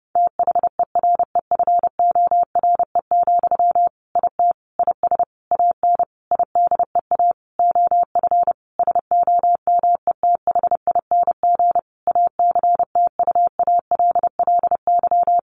Morse code, heard as a keyed tone, about 30 wpm, 700 Hz. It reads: THEREFORE, IT IS AN IDEA OF SOMETHING ACTUALLY
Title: Sentences from Top 1500 Words 30wpm